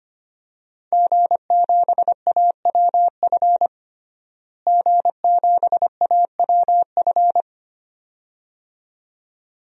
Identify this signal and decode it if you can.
Morse code, heard as a keyed tone, about 25 wpm, 700 Hz. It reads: G7AWF G7AWF
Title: DXCC-100 Call Signs - 5 Characters - Repeated 2x - 25wpm